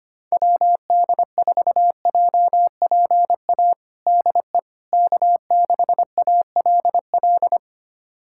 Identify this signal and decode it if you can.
Morse code, heard as a keyed tone, about 25 wpm, 700 Hz. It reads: WD4JPA DE K6ALL
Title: USA Callsign DE Callsign 25wpm